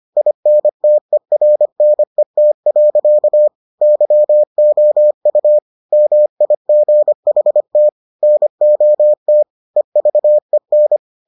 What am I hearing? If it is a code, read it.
Morse code, heard as a keyed tone, about 25 wpm, 600 Hz. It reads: INTERNET. YOU MIGHT NOT EVEN